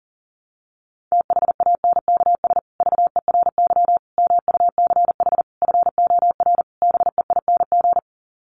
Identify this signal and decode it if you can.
Morse code, heard as a keyed tone, about 40 wpm, 700 Hz. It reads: THANKS VERY MUCH FOR BEING
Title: Sets of 5 Words Encore 40wpm